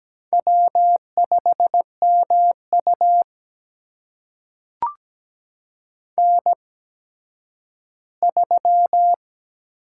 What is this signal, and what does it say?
Morse code, heard as a keyed tone, about 17 wpm, spaced as for 10 wpm, 700 Hz. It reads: W5MU N3
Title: USA Call Signs - 8x Character Spacing - 17wpm